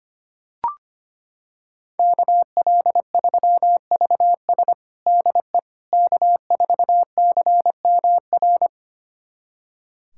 Morse code, heard as a keyed tone, about 25 wpm, 700 Hz. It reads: KL3VS DE K4CMR
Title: USA Callsign DE Callsign - 5 Characters - 25wpm